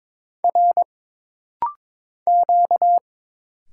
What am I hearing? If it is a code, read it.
Morse code, heard as a keyed tone, about 23 wpm, 700 Hz. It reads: R Q
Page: This rapid-fire timing is geared to push your ICR (Instant Character Recognition) skills.